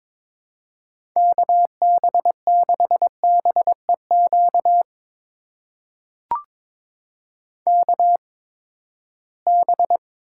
Morse code, heard as a keyed tone, about 22 wpm, spaced as for 15 wpm, 700 Hz. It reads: KB6BEQ KB
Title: USA Call Signs - 6 Characters - 8x Character Spacing - 22wpm